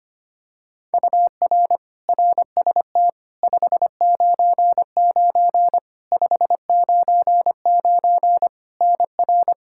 Morse code, heard as a keyed tone, about 25 wpm, 700 Hz. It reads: UR RST 599 599 NR
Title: Practice QSO - Exchange 1 - Beginning - 25wpm